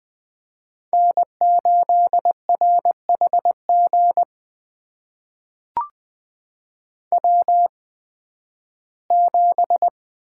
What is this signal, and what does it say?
Morse code, heard as a keyed tone, about 20 wpm, spaced as for 13 wpm, 700 Hz. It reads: N8RHG W7
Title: USA Call Signs - 8x Character Spacing - 20wpm